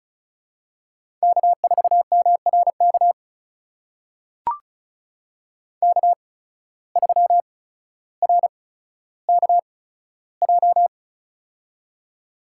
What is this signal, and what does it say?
Morse code, heard as a keyed tone, about 36 wpm, spaced as for 15 wpm, 700 Hz. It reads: K4MRK K3RKJ
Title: USA Call Signs - 8x Character Spacing - 35wpm